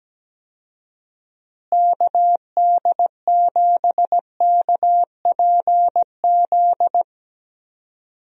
Morse code, heard as a keyed tone, about 17 wpm, 700 Hz. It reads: KD7KPZ
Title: USA Call Signs - 6 Characters - Repeated 2x - 17wpm